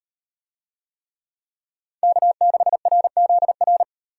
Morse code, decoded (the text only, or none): K6RZR